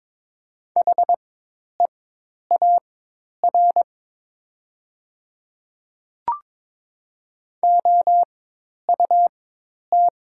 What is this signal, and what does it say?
Morse code, heard as a keyed tone, about 23 wpm, spaced as for 10 wpm, 700 Hz. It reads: HEAR OUT
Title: 4x Character Spacing - Top 300 Words - 22wpm